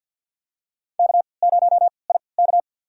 Morse code, heard as a keyed tone, about 46 wpm, spaced as for 27 wpm, 700 Hz. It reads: K0IK
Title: USA Call Signs - 4 Characters - 3x Character Spacing - 50wpm